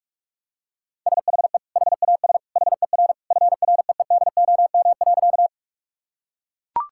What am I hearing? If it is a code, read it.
Morse code, heard as a keyed tone, about 41 wpm, 700 Hz. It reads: SHE HAS HER FREEDOM.
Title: Sentences from Top 1000 Words - Review Part 5 - 45wpm